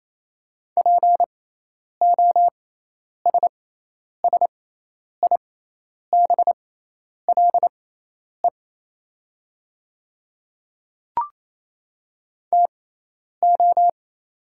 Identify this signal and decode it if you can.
Morse code, heard as a keyed tone, about 28 wpm, spaced as for 10 wpm, 700 Hz. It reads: POSSIBLE TO
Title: 6x Character Spacing - Top 500 Words - 28wpm